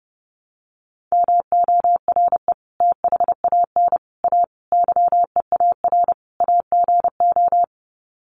Morse code, heard as a keyed tone, about 28 wpm, 700 Hz. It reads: MORE THAN A YEAR AGO
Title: Sets of 5 Words Encore2 30wpm